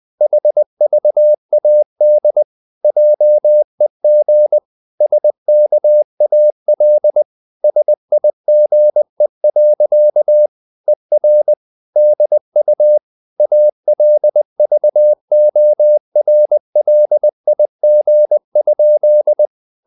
Morse code, heard as a keyed tone, about 20 wpm, 600 Hz. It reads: HVAD JEG SKAL SIGE. ER DU ALVORLIG?
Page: Ferie 20 wpm